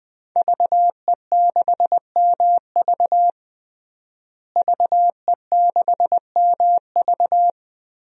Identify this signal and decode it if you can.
Morse code, heard as a keyed tone, about 21 wpm, 700 Hz. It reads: VE6MV VE6MV
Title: DXCC-100 Call Signs - 5 Characters - Repeated 3x - 20wpm